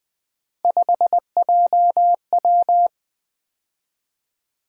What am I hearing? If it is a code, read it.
Morse code, heard as a keyed tone, about 20 wpm, 700 Hz. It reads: 5JW
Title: Three Letter-Number Combinations - 20wpm